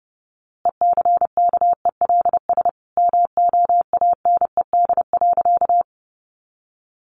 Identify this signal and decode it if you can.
Morse code, heard as a keyed tone, about 30 wpm, 700 Hz. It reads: ECKELS MOANED.
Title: A Sound Of Thunder 30wpm